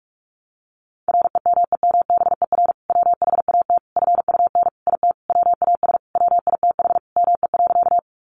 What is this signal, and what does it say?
Morse code, heard as a keyed tone, about 44 wpm, 700 Hz. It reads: REMEMBER WHAT FUN IT WAS WITH ME.
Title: Sentences from Top 700 Words 45wpm